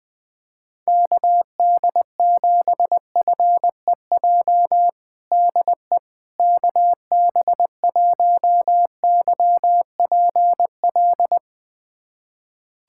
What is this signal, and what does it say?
Morse code, heard as a keyed tone, about 21 wpm, 700 Hz. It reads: KD7FEJ DE KB1YPL
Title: USA Callsign DE Callsign - 6 Characters - 20wpm